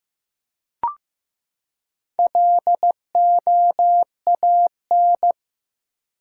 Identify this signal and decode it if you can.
Morse code, heard as a keyed tone, about 15 wpm, 700 Hz. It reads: LOAN
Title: Four Letter Words - 15wpm